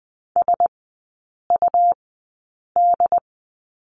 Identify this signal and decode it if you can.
Morse code, heard as a keyed tone, about 20 wpm, spaced as for 8 wpm, 700 Hz. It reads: SUD
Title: 100 Lesson 17e - Letter u - Three Characters 20wpm